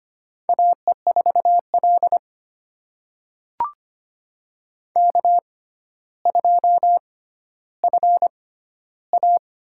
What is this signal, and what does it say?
Morse code, heard as a keyed tone, about 25 wpm, spaced as for 13 wpm, 700 Hz. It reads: AE4L K2FA
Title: USA Call Signs - 6x Character Spacing - 25wpm